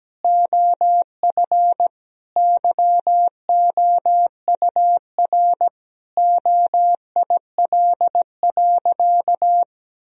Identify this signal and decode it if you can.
Morse code, heard as a keyed tone, about 17 wpm, 700 Hz. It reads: OF YOUR OIL.
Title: Sentences from Top 1000 Words - Review Part 4 - 17wpm